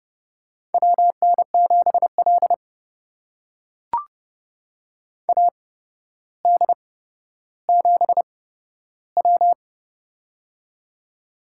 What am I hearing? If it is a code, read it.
Morse code, heard as a keyed tone, about 30 wpm, spaced as for 13 wpm, 700 Hz. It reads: WN7L AD7W
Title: USA Call Signs - 4 Characters - 8x Character Spacing - 30wpm